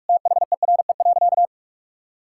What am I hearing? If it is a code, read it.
Morse code, heard as a keyed tone, about 42 wpm, 700 Hz. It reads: THERE.
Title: Sentences from Top 1000 Words - Review Part 5 - 45wpm